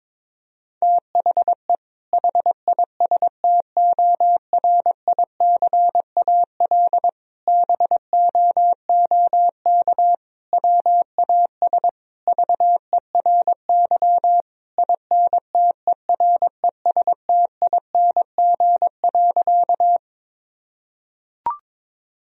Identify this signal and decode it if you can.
Morse code, heard as a keyed tone, about 22 wpm, 700 Hz. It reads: THE HISTORICAL BOOK WAS VERY INTERESTING.
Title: Sentences from Top 1000 Words - Review Part 4 - 22wpm